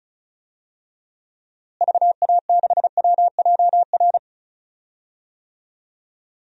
Morse code, heard as a keyed tone, about 34 wpm, 700 Hz. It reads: VA6WJR
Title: DXCC-100 Call Signs - 6 Characters - Repeated 2x - 35wpm